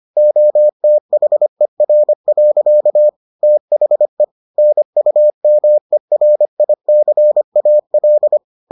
Morse code, heard as a keyed tone, about 25 wpm, 600 Hz. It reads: OTHER. THE NUMERICAL